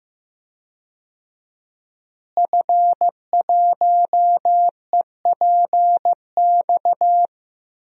Morse code, heard as a keyed tone, about 15 wpm, 700 Hz. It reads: F1EPX
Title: DXCC-100 Call Signs - 5 Characters - Repeated 2x - 15wpm